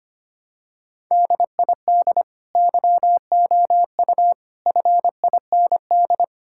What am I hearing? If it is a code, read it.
Morse code, heard as a keyed tone, about 25 wpm, 700 Hz. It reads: DID YOU FIND
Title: Sentences from Top 1000 Words - Review Part 6 - 25wpm